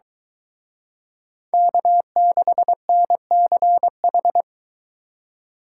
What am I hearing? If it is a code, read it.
Morse code, heard as a keyed tone, about 23 wpm, 700 Hz. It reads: K6NCH